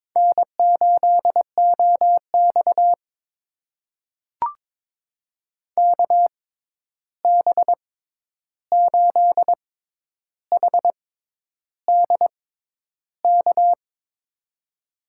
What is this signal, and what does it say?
Morse code, heard as a keyed tone, about 22 wpm, spaced as for 11 wpm, 700 Hz. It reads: N8OX KB8HD K
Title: USA Call Signs - 6x Character Spacing - 22wpm